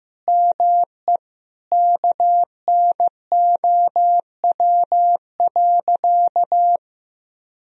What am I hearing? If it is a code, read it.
Morse code, heard as a keyed tone, about 16 wpm, 700 Hz. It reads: ME KNOW.
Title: Sentences from Top 1000 Words - Review Part 2 - 15wpm